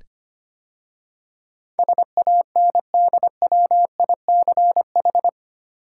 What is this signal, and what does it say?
Morse code, heard as a keyed tone, about 25 wpm, 690 Hz. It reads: SANDWICH
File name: 8 Letter Words - First 3 Letters Given - 25wpm